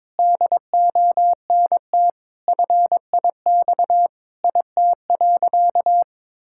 Morse code, heard as a keyed tone, about 22 wpm, 700 Hz. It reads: DONT FIX IT.
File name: Sentences with 7-Word Phrases - 22wpm